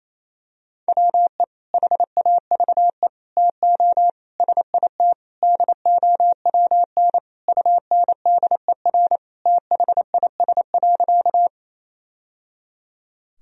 Morse code, heard as a keyed tone, about 28 wpm, 700 Hz. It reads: WE HAVE TO SIT DOWN UNDER THIS.
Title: Sentences from Top 1000 Words - Review Part 1 - 28wpm